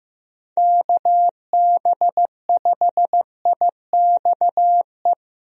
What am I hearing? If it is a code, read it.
Morse code, heard as a keyed tone, about 15 wpm, 700 Hz. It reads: KB5IXE
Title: USA Call Signs - 6 Characters - 15wpm